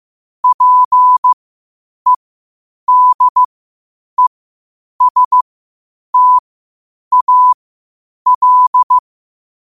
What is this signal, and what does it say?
Morse code, heard as a keyed tone, about 15 wpm, spaced as for 8 wpm, 1000 Hz. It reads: PEDESTAL
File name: qotdfiles / 8wpm